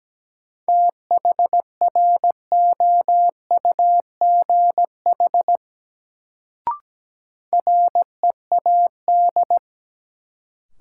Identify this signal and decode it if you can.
Morse code, heard as a keyed tone, about 17 wpm, 700 Hz. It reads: THROUGH READ
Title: Sentences from Top 200 Words - New Word Drill - 17wpm